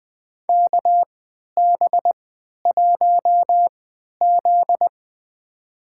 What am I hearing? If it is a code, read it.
Morse code, heard as a keyed tone, about 21 wpm, spaced as for 11 wpm, 700 Hz. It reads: KB1Z
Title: USA Call Signs - 4 Characters - 3x Character Spacing - 20wpm